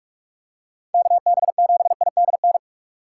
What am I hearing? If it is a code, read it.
Morse code, heard as a keyed tone, about 46 wpm, 700 Hz. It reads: KB7IDN
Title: USA Call Signs - 6x Character Spacing - 45wpm